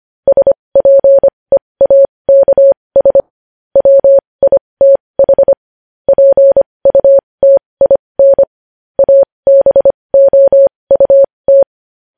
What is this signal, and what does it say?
Morse code, heard as a keyed tone, about 25 wpm, spaced as for 20 wpm, 550 Hz. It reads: SPEAKS WITH PUTIN ABOUT
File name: podcast / News headlines-1x-25x20